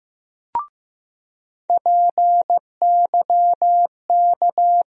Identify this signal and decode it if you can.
Morse code, heard as a keyed tone, about 15 wpm, 700 Hz. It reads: PYK
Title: USA Call Sign Postfixes - 3 Letters 15wpm